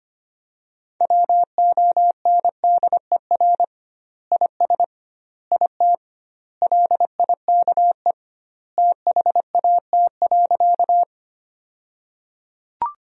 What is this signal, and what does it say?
Morse code, heard as a keyed tone, about 25 wpm, 700 Hz. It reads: WONDER IS IT LIKE THAT.
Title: 2x Word-Spacing - Sentences from Top 500 Words - 25wpm